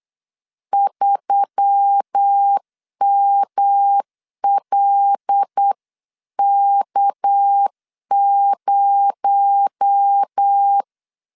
Morse code, heard as a keyed tone, about 8 wpm, 800 Hz. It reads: 3MLK0
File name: 8wpm